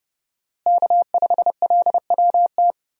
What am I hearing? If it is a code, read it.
Morse code, heard as a keyed tone, about 31 wpm, 700 Hz. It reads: K5LWT
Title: USA Call Signs - 5 Characters - Repeated 3x - 30wpm